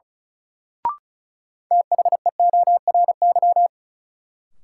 Morse code, heard as a keyed tone, about 36 wpm, 700 Hz. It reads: THEORY
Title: Sentences from Top 600 Words - New Word Drill - 35wpm